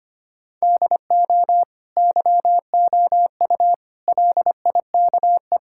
Morse code, heard as a keyed tone, about 25 wpm, 700 Hz. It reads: DO YOU LIKE
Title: Sentences from Top 1000 Words - Review Part 5 - 25wpm